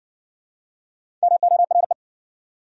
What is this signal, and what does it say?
Morse code, heard as a keyed tone, about 60 wpm, 700 Hz. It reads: MORE